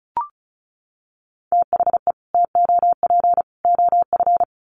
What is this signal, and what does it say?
Morse code, heard as a keyed tone, about 34 wpm, 700 Hz. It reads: THE TOP OF
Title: Sets of 3 Words 35wpm